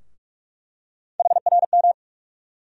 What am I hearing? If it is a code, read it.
Morse code, heard as a keyed tone, about 47 wpm, 700 Hz. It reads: HRM
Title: USA Call Sign Postfixes - 3 Letters 45wpm